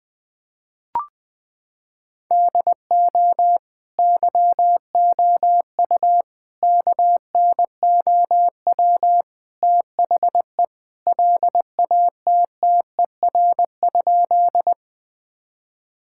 Morse code, heard as a keyed tone, about 20 wpm, 700 Hz. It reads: DO YOU KNOW THE LATTER?
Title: Sentences from Top 1600 Words 20wpm